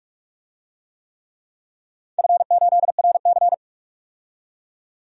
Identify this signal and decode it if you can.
Morse code, heard as a keyed tone, about 44 wpm, 700 Hz. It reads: F8RC
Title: DXCC-100 Call Signs - 4 Characters - 45wpm